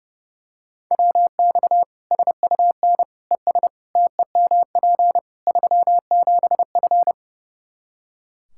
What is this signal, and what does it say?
Morse code, heard as a keyed tone, about 30 wpm, 700 Hz. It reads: WX SUN ES TEMP 37F
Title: Practice QSO - Exchange 2 - Info - 30wpm